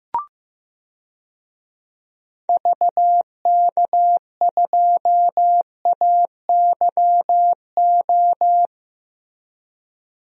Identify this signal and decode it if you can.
Morse code, heard as a keyed tone, about 16 wpm, 700 Hz. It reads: VK2AYO
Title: DXCC-100 Call Signs - 6 Characters - Repeated 3x - 15wpm